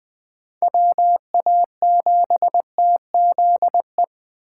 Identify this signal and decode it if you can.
Morse code, heard as a keyed tone, about 20 wpm, 700 Hz. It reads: WA7TZE